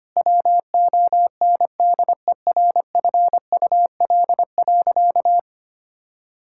Morse code, heard as a keyed tone, about 25 wpm, 700 Hz. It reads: WONDERFUL.
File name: practice / 25wpm / Sentences with 3-Word Phrases - 25wpm